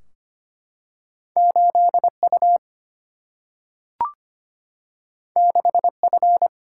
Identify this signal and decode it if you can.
Morse code, heard as a keyed tone, about 25 wpm, 700 Hz. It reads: 8 U 6 F
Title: Single-Letter Single-Number 25wpm